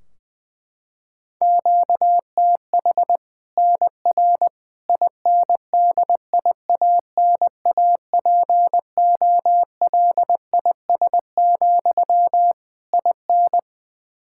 Morse code, heard as a keyed tone, about 21 wpm, 700 Hz. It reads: QTH NR INDIANAPOLIS, IN
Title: Practice QSO - Exchange 1 - Beginning - 20wpm